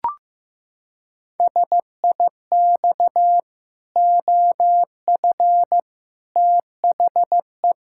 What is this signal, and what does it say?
Morse code, heard as a keyed tone, about 15 wpm, 700 Hz. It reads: SIX OF THE
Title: Pangram Challenge v2 - 15wpm